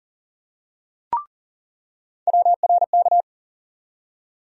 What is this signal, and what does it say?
Morse code, heard as a keyed tone, about 40 wpm, 700 Hz. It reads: WRK
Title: USA Call Sign Postfixes - 3 Letters 40wpm